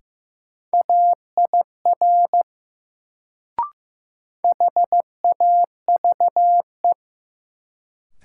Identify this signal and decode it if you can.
Morse code, heard as a keyed tone, about 15 wpm, 700 Hz. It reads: AIR HAVE
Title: Sets of 2 Words - New Word Drill - 15wpm